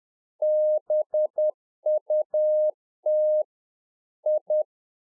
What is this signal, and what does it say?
Morse code, heard as a keyed tone, about 10 wpm, 610 Hz. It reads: BUT I
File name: 2011-04-mp3 / RICK JJ0MPI 10WPM